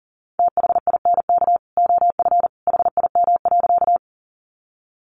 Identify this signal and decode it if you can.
Morse code, heard as a keyed tone, about 40 wpm, 700 Hz. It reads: THINK OF HIM.
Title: Sentences from Top 200 Words 40wpm